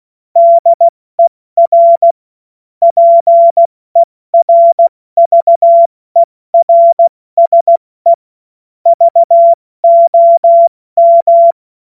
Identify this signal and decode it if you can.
Morse code, heard as a keyed tone, about 16 wpm, 680 Hz. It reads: DER PERVERSE VOM
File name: Morsekurs / Witze / Witz 24. Januar 2023 70 F80